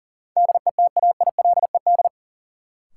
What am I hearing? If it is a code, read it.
Morse code, heard as a keyed tone, about 40 wpm, 700 Hz. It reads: DETAILED
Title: Sentences from Top 1500 Words - New Word Drill - 40wpm